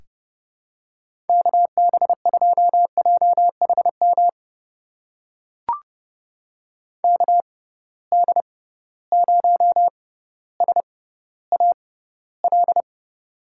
Morse code, heard as a keyed tone, about 30 wpm, spaced as for 15 wpm, 700 Hz. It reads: KB2JHM KD0SAL
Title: USA Call Signs - 6 Characters - 6x Character Spacing - 30wpm